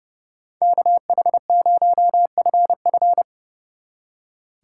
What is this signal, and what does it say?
Morse code, heard as a keyed tone, about 37 wpm, 700 Hz. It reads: KH0FF
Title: Speed Racing - USA 2x2 Call Signs 30wpm